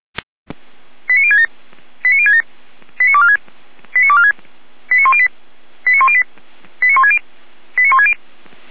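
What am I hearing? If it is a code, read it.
Five-tone selective call (ZVEI), heard as quick runs of five tones, tones of about 70 ms. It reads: 89067 89067 89246 89246 89108 89108 89160 89160
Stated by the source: ZVEI five-tone selective call 89170 received on SDR.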